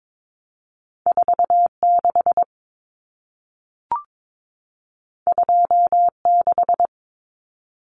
Morse code, heard as a keyed tone, about 22 wpm, 700 Hz. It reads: 46 26
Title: Common Cut Numbers - 2-Digit 22wpm